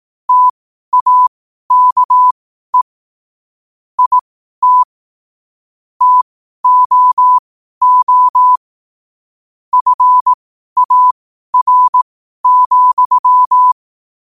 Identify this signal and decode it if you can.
Morse code, heard as a keyed tone, about 18 wpm, spaced as for 12 wpm, 1000 Hz. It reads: TAKE IT TOO FAR,
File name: qotdfiles / 12wpm